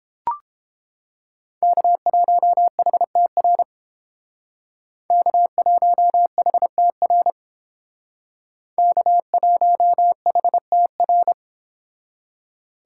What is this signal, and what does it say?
Morse code, heard as a keyed tone, about 30 wpm, 700 Hz. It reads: K1HTR K1HTR K1HTR